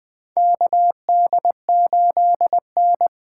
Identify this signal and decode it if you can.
Morse code, heard as a keyed tone, about 20 wpm, 700 Hz. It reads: KD8N
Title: USA Call Signs - 6 Characters - 8x Character Spacing - 20wpm